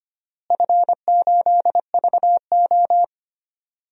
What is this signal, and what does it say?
Morse code, heard as a keyed tone, about 25 wpm, 700 Hz. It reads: F8VO
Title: DXCC-100 Call Signs - 4 Characters - Repeated 3x - 25wpm